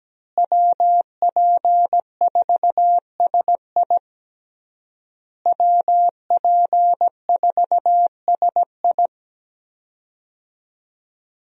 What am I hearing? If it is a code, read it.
Morse code, heard as a keyed tone, about 17 wpm, 700 Hz. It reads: WP4SI WP4SI
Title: USA Call Signs - 5 Characters - Repeated 3x - 17wpm